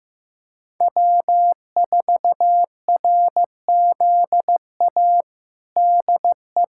Morse code, heard as a keyed tone, about 15 wpm, 700 Hz. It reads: W4RZA DE
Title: Practice QSO - Exchange 4 - Ending - 15wpm